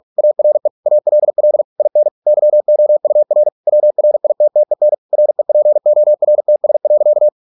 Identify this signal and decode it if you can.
Morse code, heard as a keyed tone, about 46 wpm, 600 Hz. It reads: ARE ALL IN YOUR WRITTEN REPORTS.